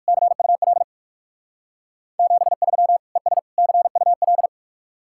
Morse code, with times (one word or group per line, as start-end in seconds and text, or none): 0.1-0.8 CUL
2.2-3.0 73
3.2-3.4 ES
3.6-4.5 CUL